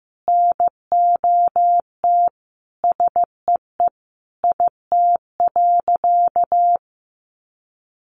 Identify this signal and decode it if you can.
Morse code, heard as a keyed tone, about 15 wpm, 700 Hz. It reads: NOT SEE IT.
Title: Sentences from Top 1000 Words 15wpm